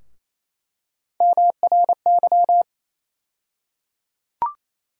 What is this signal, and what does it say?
Morse code, heard as a keyed tone, about 28 wpm, 700 Hz. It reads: MRY
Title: USA Call Sign Postfixes - 3 Letters 28wpm